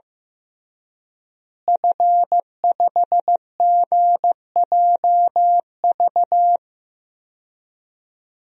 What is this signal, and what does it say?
Morse code, heard as a keyed tone, about 14 wpm, 700 Hz. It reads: F5GJV
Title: DXCC-100 Call Signs - 5 Characters - Repeated 3x - 15wpm